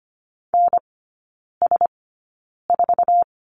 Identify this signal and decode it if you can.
Morse code, heard as a keyed tone, about 25 wpm, spaced as for 9 wpm, 700 Hz. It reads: NS4
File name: Lesson 17e - Letter u - Three Characters 25wpm